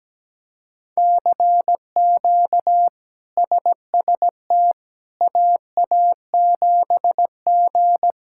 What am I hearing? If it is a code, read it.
Morse code, heard as a keyed tone, about 17 wpm, 700 Hz. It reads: CQ SST AA7G
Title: K1USN SST QSO Practice - 17wpm